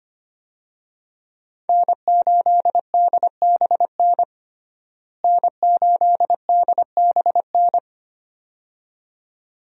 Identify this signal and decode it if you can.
Morse code, heard as a keyed tone, about 25 wpm, 700 Hz. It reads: N8DBN N8DBN
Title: USA Call Signs - 5 Characters - Repeated 2x - 25wpm